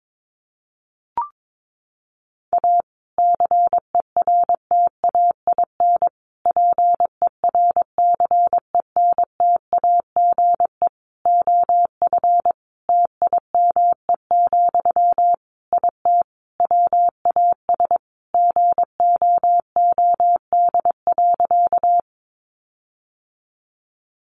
Morse code, heard as a keyed tone, about 22 wpm, 700 Hz. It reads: A CERTAIN PERCENTAGE OF TIME, IT WAS GOOD.
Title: Sentences from Top 1300 Words 22wpm